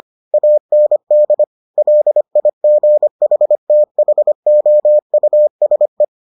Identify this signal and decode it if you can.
Morse code, heard as a keyed tone, about 25 wpm, 600 Hz. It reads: AND LIGHTHOUSE